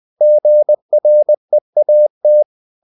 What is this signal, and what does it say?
Morse code, heard as a keyed tone, about 20 wpm, 600 Hz. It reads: GREAT
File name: the old man and the sea part 2 20wpm